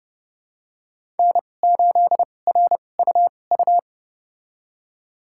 Morse code, heard as a keyed tone, about 30 wpm, spaced as for 22 wpm, 700 Hz. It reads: N8RUU
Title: USA Call Signs - 5 Characters - 2x Character Spacing - 30wpm